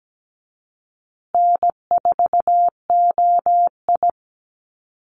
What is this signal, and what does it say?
Morse code, heard as a keyed tone, about 17 wpm, 700 Hz. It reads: N4OI
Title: Easy Call Signs Repeated 3x - 17wpm